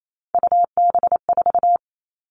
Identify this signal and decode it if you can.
Morse code, heard as a keyed tone, about 28 wpm, 700 Hz. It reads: UB4
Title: Three Challenging Letter-Number Combinations 28wpm